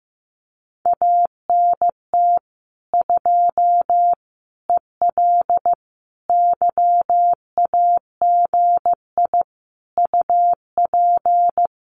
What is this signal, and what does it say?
Morse code, heard as a keyed tone, about 15 wpm, 700 Hz. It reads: ANT 2 EL YAGI UP
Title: Common QSO Phrases Encore 15wpm